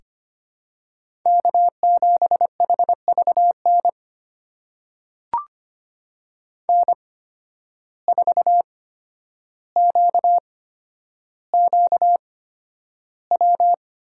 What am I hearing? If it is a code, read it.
Morse code, heard as a keyed tone, about 25 wpm, spaced as for 11 wpm, 700 Hz. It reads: K7HVN N4QQW
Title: USA Call Signs - 5 Characters - 8x Character Spacing - 25wpm